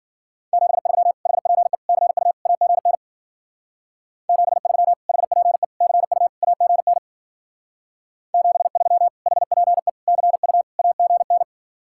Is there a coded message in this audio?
Yes: Morse code from 0.5 to 11.4 s, heard as a keyed tone, about 51 wpm, 700 Hz. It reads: 73 HPE CU AGN 73 HPE CU AGN 73 HPE CU AGN